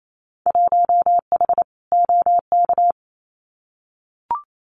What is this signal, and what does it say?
Morse code, heard as a keyed tone, about 28 wpm, 700 Hz. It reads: 1H OK
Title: Winter Field Day Exchanges 28wpm